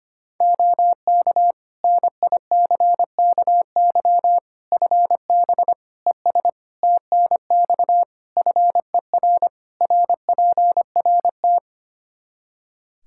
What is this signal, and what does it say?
Morse code, heard as a keyed tone, about 25 wpm, 700 Hz. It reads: OK NICKY FB ES TNX FER RPRT
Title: Practice QSO - Exchange 2 - Info - 25wpm